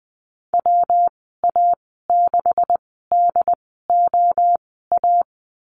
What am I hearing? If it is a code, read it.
Morse code, heard as a keyed tone, about 20 wpm, spaced as for 15 wpm, 700 Hz. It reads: WA6DOA
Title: 167 Lesson 27g - Number 7 - Call Signs 20wpm